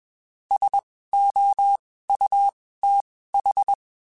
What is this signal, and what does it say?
Morse code, heard as a keyed tone, about 21 wpm, spaced as for 15 wpm, 800 Hz. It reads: SOUTH